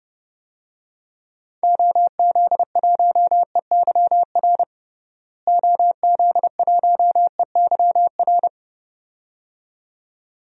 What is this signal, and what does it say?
Morse code, heard as a keyed tone, about 32 wpm, 700 Hz. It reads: OZ1EYR OZ1EYR
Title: DXCC-100 Call Signs - 6 Characters - Repeated 2x - 30wpm